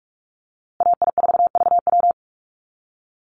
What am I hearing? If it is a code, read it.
Morse code, heard as a keyed tone, about 44 wpm, 700 Hz. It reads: AI4VW
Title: USA Call Signs Repeated 3x 45wpm